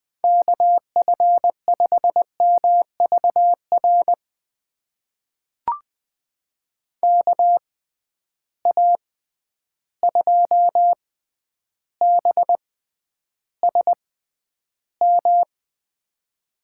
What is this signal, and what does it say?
Morse code, heard as a keyed tone, about 20 wpm, spaced as for 10 wpm, 700 Hz. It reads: KF5MVR KA2BSM
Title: USA Call Signs - 6 Characters - 6x Character Spacing - 20wpm